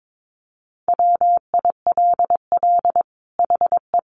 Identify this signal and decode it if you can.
Morse code, heard as a keyed tone, about 22 wpm, 700 Hz. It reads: WILL HE
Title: Sentences from Top 1400 Words 22wpm